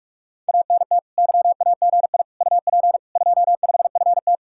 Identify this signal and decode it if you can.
Morse code, heard as a keyed tone, about 46 wpm, 700 Hz. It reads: ANT YAGI UP 25FT